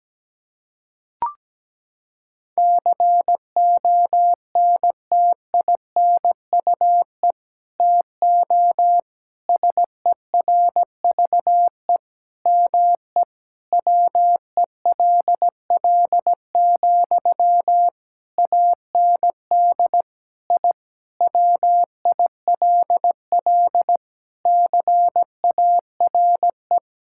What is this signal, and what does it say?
Morse code, heard as a keyed tone, about 17 wpm, 700 Hz. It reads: CONTINUE TO SERVE ME WELL, AND I WILL CARE
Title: Sentences from Top 1000 Words - Review Part 6 - 17wpm